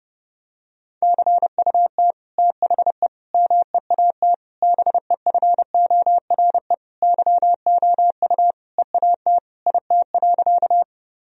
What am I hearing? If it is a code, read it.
Morse code, heard as a keyed tone, about 30 wpm, 700 Hz. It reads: CUT THE MEAT BEFORE YOU EAT IT.
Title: Sentences from Top 1000 Words - Review Part 3 - 30wpm